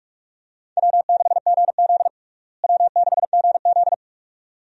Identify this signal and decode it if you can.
Morse code, heard as a keyed tone, about 46 wpm, 700 Hz. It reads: W6GZ W6GZ
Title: USA Call Signs - 4 Characters - Repeated 2x - 45wpm